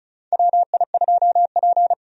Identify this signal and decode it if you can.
Morse code, heard as a keyed tone, about 36 wpm, 700 Hz. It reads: WI2P
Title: USA Call Signs - 4 Characters - 35wpm